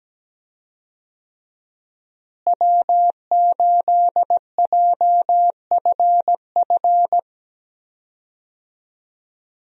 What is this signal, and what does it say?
Morse code, heard as a keyed tone, about 17 wpm, 700 Hz. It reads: W8JFF
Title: USA Call Signs - 5 Characters - Repeated 3x - 17wpm